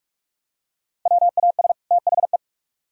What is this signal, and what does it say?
Morse code, heard as a keyed tone, about 42 wpm, 700 Hz. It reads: WAS THE